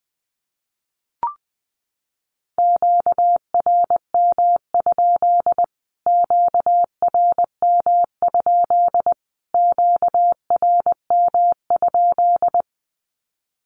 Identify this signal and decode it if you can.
Morse code, heard as a keyed tone, about 20 wpm, 700 Hz. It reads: QRM? QRM? QRM?
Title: Q-Codes Repeated 3x 20wpm